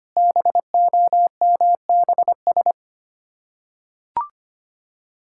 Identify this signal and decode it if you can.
Morse code, heard as a keyed tone, about 25 wpm, 700 Hz. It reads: BOMBS
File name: practice / 25wpm / Difficult Words with Hints - 25wpm